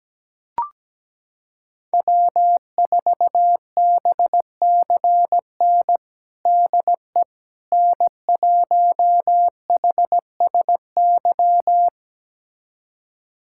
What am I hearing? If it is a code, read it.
Morse code, heard as a keyed tone, about 17 wpm, 700 Hz. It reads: W4BCN DE N1HSY
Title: USA Callsign DE Callsign - 5 Characters - 17wpm